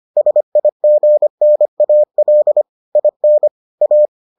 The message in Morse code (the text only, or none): SIGNAL IN A